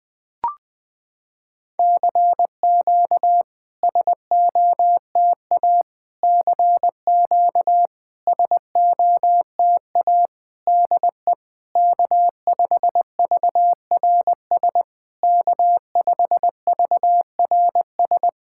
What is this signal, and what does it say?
Morse code, heard as a keyed tone, about 20 wpm, 700 Hz. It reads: CQ SOTA CQ SOTA DE K5VRS K5VRS
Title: SOTA QSO Practice 20wpm